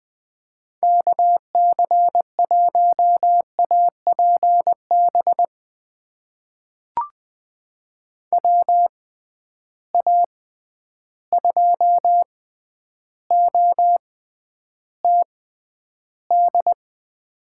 Morse code, heard as a keyed tone, about 20 wpm, spaced as for 10 wpm, 700 Hz. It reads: KC1APB WA2OTD
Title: USA Call Signs - 6 Characters - 6x Character Spacing - 20wpm